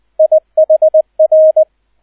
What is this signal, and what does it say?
Morse code, heard as a keyed tone, about 20 wpm, 630 Hz, with radio noise. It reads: IHR
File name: Erster Streich 20wpm